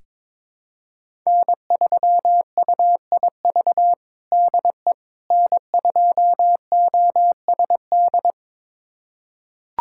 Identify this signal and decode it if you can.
Morse code, heard as a keyed tone, about 22 wpm, 700 Hz. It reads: N3UIV DE N2OSD
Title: USA Callsign DE Callsign - 5 Characters - 22wpm